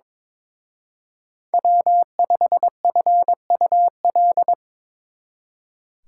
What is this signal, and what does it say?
Morse code, heard as a keyed tone, about 24 wpm, 700 Hz. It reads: W5FUL
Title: Speed Racing - USA 1x3 Call Signs 22wpm